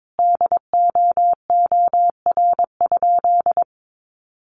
Morse code, heard as a keyed tone, about 22 wpm, 700 Hz. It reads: DOOR?
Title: Sentences from Top 900 Words 22wpm